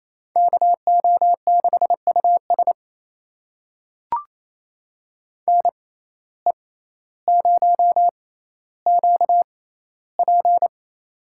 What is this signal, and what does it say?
Morse code, heard as a keyed tone, about 28 wpm, spaced as for 14 wpm, 700 Hz. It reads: KO6US NE0QP
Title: USA Call Signs - 6x Character Spacing - 28wpm